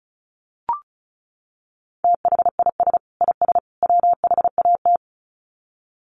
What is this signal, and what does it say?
Morse code, heard as a keyed tone, about 34 wpm, 700 Hz. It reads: THIS IS WHAT
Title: Sets of 3 Words 35wpm